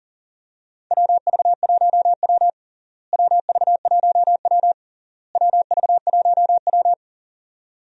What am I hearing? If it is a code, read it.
Morse code, heard as a keyed tone, about 37 wpm, 700 Hz. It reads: WV1W WV1W WV1W
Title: USA Call Signs - 4 Characters - Repeated 3x - 40wpm